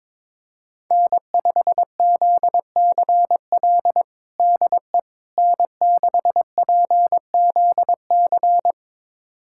Morse code, heard as a keyed tone, about 21 wpm, 700 Hz. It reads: N5ZCL DE N6PZC
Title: USA Callsign DE Callsign 22wpm